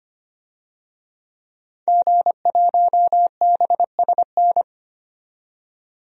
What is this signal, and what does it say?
Morse code, heard as a keyed tone, about 25 wpm, 700 Hz. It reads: G1BSN
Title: DXCC-100 Call Signs - 5 Characters - 25wpm